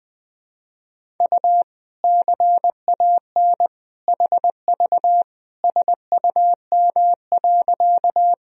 Morse code, heard as a keyed tone, about 21 wpm, 700 Hz. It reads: U CAN HV SUM.
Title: Sentences from Top 100 Words and Abbreviations 20wpm